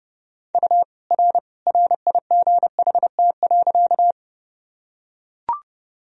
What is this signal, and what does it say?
Morse code, heard as a keyed tone, about 31 wpm, 700 Hz. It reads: U R RIGHT.
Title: Sentences from Top 200 Words and Abbreviations 30wpm